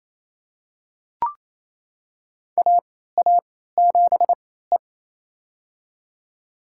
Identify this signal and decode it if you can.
Morse code, heard as a keyed tone, about 28 wpm, spaced as for 16 wpm, 700 Hz. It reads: AA7E
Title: USA Call Signs - 4 Characters - 3x Character Spacing - 28wpm